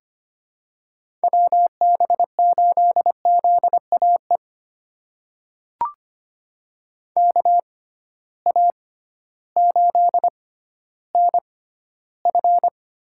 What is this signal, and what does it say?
Morse code, heard as a keyed tone, about 25 wpm, spaced as for 13 wpm, 700 Hz. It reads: WB8ZAE KA8NF
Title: USA Call Signs - 6 Characters - 6x Character Spacing - 25wpm